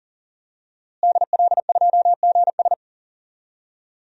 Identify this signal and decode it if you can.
Morse code, heard as a keyed tone, about 37 wpm, 700 Hz. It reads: DL2GS
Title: DXCC-100 Call Signs - 5 Characters - Repeated 3x - 40wpm